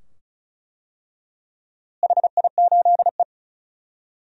Morse code, heard as a keyed tone, about 34 wpm, 700 Hz. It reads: HI8E
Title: DXCC-100 Call Signs - 4 Characters - Repeated 2x - 35wpm